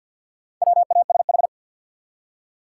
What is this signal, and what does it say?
Morse code, heard as a keyed tone, about 47 wpm, 700 Hz. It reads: WASH
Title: Four Letter Words - 50wpm